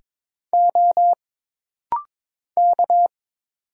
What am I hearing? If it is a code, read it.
Morse code, heard as a keyed tone, about 22 wpm, 700 Hz. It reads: O K
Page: This rapid-fire timing is geared to push your ICR (Instant Character Recognition) skills.